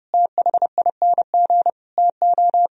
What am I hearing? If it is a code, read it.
Morse code, heard as a keyed tone, about 37 wpm, 700 Hz. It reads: THING TO
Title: Speed-Racing - Sets of 5 Words 30wpm